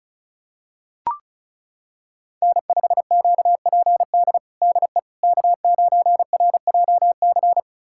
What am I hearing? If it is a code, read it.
Morse code, heard as a keyed tone, about 34 wpm, 700 Hz. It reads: N5QPD DE K9RJC
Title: USA Callsign DE Callsign - 5 Characters - 35wpm